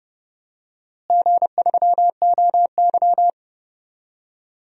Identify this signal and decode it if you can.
Morse code, heard as a keyed tone, about 30 wpm, 700 Hz. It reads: G3OY
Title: DXCC-100 Call Signs - 4 Characters - Repeated 3x - 30wpm